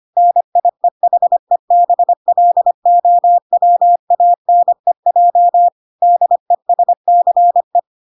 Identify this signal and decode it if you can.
Morse code, heard as a keyed tone, about 25 wpm, 700 Hz. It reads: NIEHEBLOWANEJ DESCE